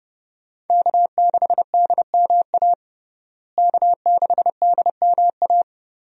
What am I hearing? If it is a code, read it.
Morse code, heard as a keyed tone, about 31 wpm, 700 Hz. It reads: K6DMA K6DMA
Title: USA Call Signs - 5 Characters - Repeated 2x - 30wpm